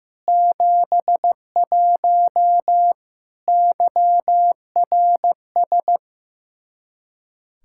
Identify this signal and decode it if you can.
Morse code, heard as a keyed tone, about 15 wpm, 700 Hz. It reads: 71 YRS
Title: Practice QSO - Exchange 3 - More Info - 15wpm